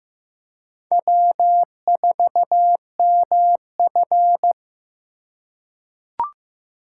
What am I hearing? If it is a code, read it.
Morse code, heard as a keyed tone, about 15 wpm, 700 Hz. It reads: W4MF
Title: USA Call Signs - 6x Character Spacing - 15wpm